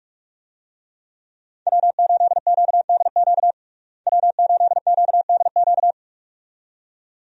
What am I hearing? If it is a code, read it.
Morse code, heard as a keyed tone, about 44 wpm, 700 Hz. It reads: W8QDQ W8QDQ
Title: USA Call Signs - 5 Characters - Repeated 2x - 45wpm